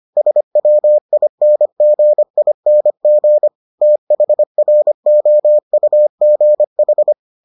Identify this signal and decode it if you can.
Morse code, heard as a keyed tone, about 25 wpm, 600 Hz. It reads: SWINGING THROUGH